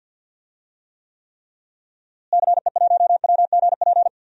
Morse code, heard as a keyed tone, about 46 wpm, 700 Hz. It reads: CE1WGP
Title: DXCC-100 Call Signs - 6 Characters - Repeated 3x - 50wpm